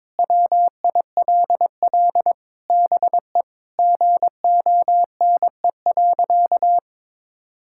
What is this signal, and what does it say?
Morse code, heard as a keyed tone, about 21 wpm, 700 Hz. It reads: WILL BE GONE.
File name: practice / 22wpm / Sentences with 4-Word Phrases - 22wpm